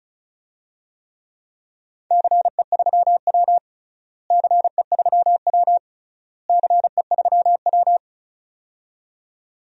Morse code, heard as a keyed tone, about 36 wpm, 700 Hz. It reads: CE3W CE3W CE3W
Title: DXCC-100 Call Signs - 4 Characters - Repeated 3x - 35wpm